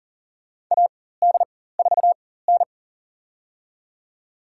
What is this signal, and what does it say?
Morse code, heard as a keyed tone, about 39 wpm, spaced as for 18 wpm, 700 Hz. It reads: AD4N
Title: USA Call Signs - 4x Character Spacing - 40wpm